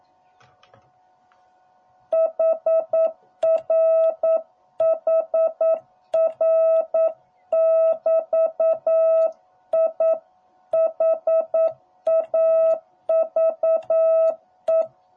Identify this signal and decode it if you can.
Morse code, heard as a keyed tone, about 9 wpm, 660 Hz. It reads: HR HR <BT> I HAVE